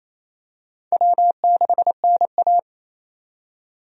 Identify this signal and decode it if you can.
Morse code, heard as a keyed tone, about 28 wpm, 700 Hz. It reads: W6NA
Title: USA Call Signs - 4 Characters - 28wpm